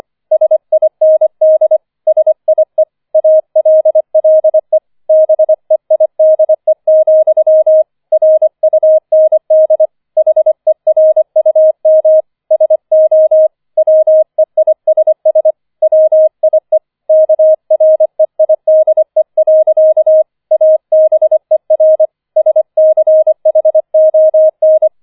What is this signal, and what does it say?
Morse code, heard as a keyed tone, about 25 wpm, 630 Hz. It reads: SIND SIE ALLE BEIDE, RUND HERUM SO WEISS WIE KREIDE. ABER SCHON